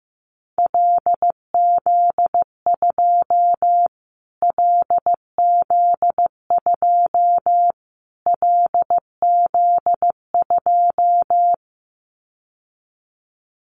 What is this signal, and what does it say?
Morse code, heard as a keyed tone, about 15 wpm, 700 Hz. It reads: LZ2 LZ2 LZ2
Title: Callsign Prefixes for Top 25 DXCC - Repeated 3x - 15wpm